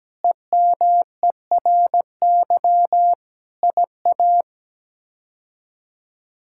Morse code, heard as a keyed tone, about 17 wpm, 700 Hz. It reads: EMERY IA
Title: K1USN SST QSO Practice - 17wpm